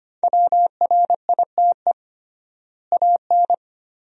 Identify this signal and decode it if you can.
Morse code, heard as a keyed tone, about 25 wpm, 700 Hz. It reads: WRITE AN
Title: 3x Word-Spacing - Sentences from Top 300 Words - 25wpm